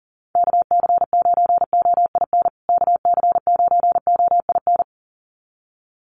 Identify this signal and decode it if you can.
Morse code, heard as a keyed tone, about 40 wpm, 700 Hz. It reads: KC9OIN KC9OIN
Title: USA Call Signs Repeated 2x 40wpm